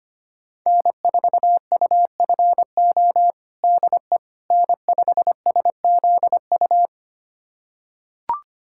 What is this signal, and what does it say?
Morse code, heard as a keyed tone, about 25 wpm, 700 Hz. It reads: N4UFO DE N5SZU
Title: USA Callsign DE Callsign - 5 Characters - 25wpm